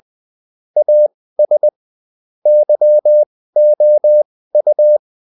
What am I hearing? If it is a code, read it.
Morse code, heard as a keyed tone, about 20 wpm, spaced as for 15 wpm, 600 Hz. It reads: AS YOU
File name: Top-100-Words-15